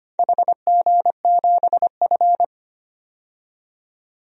Morse code, heard as a keyed tone, about 25 wpm, 700 Hz. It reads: HG7F
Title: DXCC-100 Call Signs - 4 Characters - 25wpm